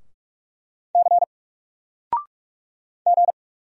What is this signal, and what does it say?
Morse code, heard as a keyed tone, about 44 wpm, 700 Hz. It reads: C G
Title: Single Letter-Number 45wpm